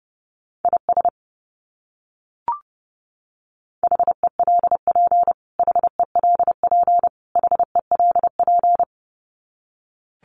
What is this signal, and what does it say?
Morse code, heard as a keyed tone, about 30 wpm, 700 Hz. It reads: IS HELP HELP HELP
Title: Top 200 Words Repeated 30wpm